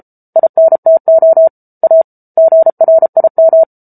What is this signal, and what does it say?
Morse code, heard as a keyed tone, about 34 wpm, 650 Hz. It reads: INTO A GRIM